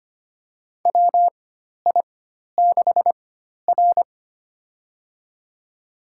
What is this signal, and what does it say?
Morse code, heard as a keyed tone, about 25 wpm, spaced as for 12 wpm, 690 Hz. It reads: WI6R
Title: USA Call Signs - 4 Characters - 4x Character Spacing - 25wpm